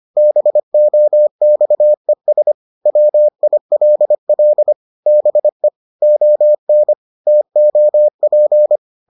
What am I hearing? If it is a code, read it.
Morse code, heard as a keyed tone, about 25 wpm, 600 Hz. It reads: BOXES WILL BE ON TOP